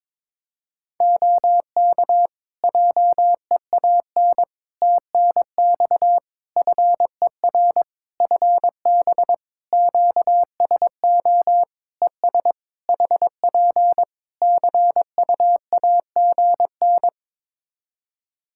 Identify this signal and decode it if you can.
Morse code, heard as a keyed tone, about 22 wpm, 700 Hz. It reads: OK JEAN TNX FER FB QSO ES HP CUAGN
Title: Practice QSO - Exchange 4 - Ending - 22wpm